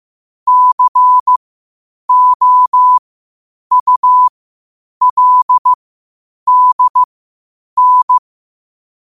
Morse code, heard as a keyed tone, about 15 wpm, spaced as for 8 wpm, 1000 Hz. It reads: COULDN
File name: qotdfiles / 8wpm